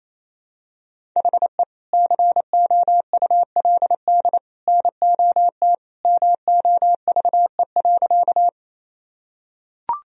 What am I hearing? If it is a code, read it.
Morse code, heard as a keyed tone, about 28 wpm, 700 Hz. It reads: HE COULD NOT MOVE.
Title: Sentences from Top 1000 Words - Review Part 1 - 28wpm